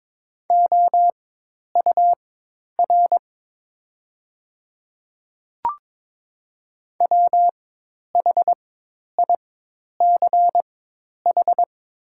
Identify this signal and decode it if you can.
Morse code, heard as a keyed tone, about 22 wpm, spaced as for 10 wpm, 700 Hz. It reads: OUR WHICH
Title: 4x Character Spacing - Top 300 Words - 22wpm